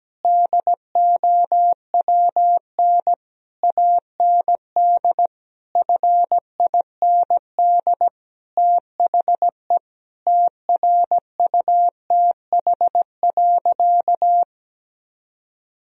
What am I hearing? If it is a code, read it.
Morse code, heard as a keyed tone, about 17 wpm, 700 Hz. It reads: DOWN AND FIND THE TRUTH.
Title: Sentences from Top 1000 Words - Review Part 3 - 17wpm